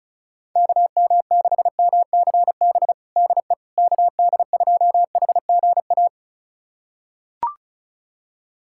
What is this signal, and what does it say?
Morse code, heard as a keyed tone, about 36 wpm, 700 Hz. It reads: KM6MCB DE KD2HGA
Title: USA Callsign DE Callsign 35wpm